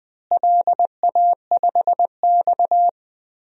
Morse code, heard as a keyed tone, about 20 wpm, 700 Hz. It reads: LA5X
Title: DXCC-100 Call Signs - 4 Characters - Repeated 3x - 20wpm